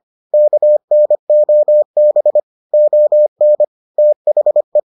Morse code, heard as a keyed tone, about 25 wpm, 600 Hz. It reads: KNOB ON THE